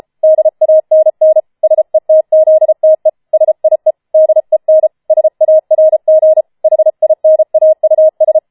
Morse code, heard as a keyed tone, about 33 wpm, 630 Hz. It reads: DANN SETZTE SIE DEN SARG HINAUS
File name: Schneewittchen Teil5 34wpm